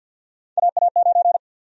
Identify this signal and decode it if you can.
Morse code, heard as a keyed tone, about 49 wpm, 700 Hz. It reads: AA9
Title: USA Call Sign Prefixes - 50wpm